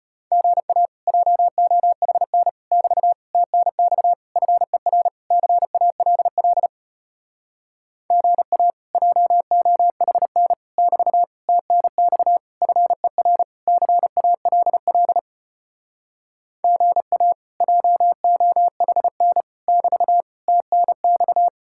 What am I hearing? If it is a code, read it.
Morse code, heard as a keyed tone, about 33 wpm, 700 Hz. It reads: GA JOHN <BT> TNX FER CALL GA JOHN <BT> TNX FER CALL GA JOHN <BT> TNX